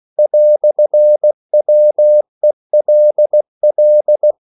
Morse code, heard as a keyed tone, about 16 wpm, 600 Hz. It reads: "WELL
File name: CW / Practice / EnglishQuotations0416